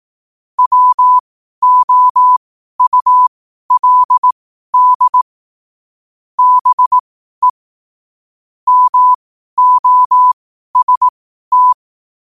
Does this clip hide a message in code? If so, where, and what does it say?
Yes: Morse code from 0.6 to 11.7 s, heard as a keyed tone, about 18 wpm, spaced as for 12 wpm, 1000 Hz. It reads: WOULD BE MOST